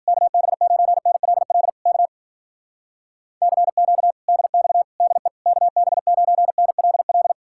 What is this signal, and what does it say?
Morse code, heard as a keyed tone, about 52 wpm, 700 Hz. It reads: KB9NLL K CQ DX DE KB9NLL